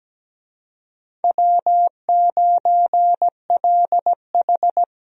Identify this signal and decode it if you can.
Morse code, heard as a keyed tone, about 17 wpm, 700 Hz. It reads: W9LH
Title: USA Call Signs - 4 Characters - 17wpm